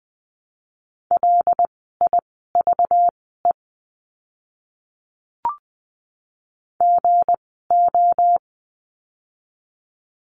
Morse code, heard as a keyed tone, about 20 wpm, spaced as for 15 wpm, 700 Hz. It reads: LIVE GO
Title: 192 Lesson 31f - Letter v - Words 20wpm